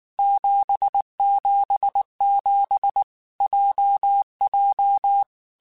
Morse code, heard as a keyed tone, about 19 wpm, 800 Hz. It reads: 777 JJ
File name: Basic Sets / Basic Letter Groups Collective7 / JZ17@18